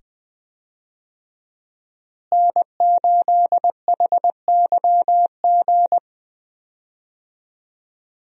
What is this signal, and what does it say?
Morse code, heard as a keyed tone, about 20 wpm, 700 Hz. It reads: N8HYG
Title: USA Call Signs - 5 Characters - Repeated 2x - 20wpm